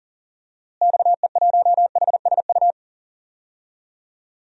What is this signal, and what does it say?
Morse code, heard as a keyed tone, about 40 wpm, 700 Hz. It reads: XE1HSU
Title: DXCC-100 Call Signs - 6 Characters - Repeated 3x - 40wpm